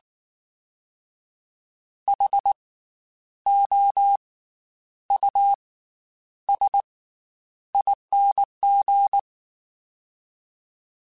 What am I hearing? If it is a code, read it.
Morse code, heard as a keyed tone, about 19 wpm, spaced as for 9 wpm, 800 Hz. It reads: HOUSING
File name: Special INGs / Special INGs-mp3s / Slow ing @18